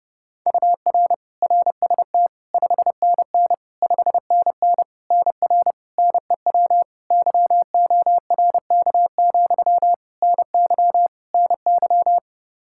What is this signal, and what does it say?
Morse code, heard as a keyed tone, about 32 wpm, 700 Hz. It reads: UR RST 5NN 5NN NR NEW YORK, NY NY
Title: Practice QSO - Exchange 1 - Beginning - 30wpm